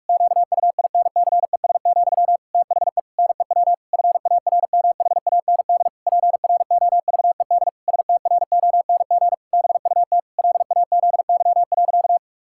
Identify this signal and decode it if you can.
Morse code, heard as a keyed tone, about 46 wpm, 700 Hz. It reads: QUINCES, THE NEW FARMHAND PROVED STRONG BUT LAZY.
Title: Pangram Challenge v2 - 45wpm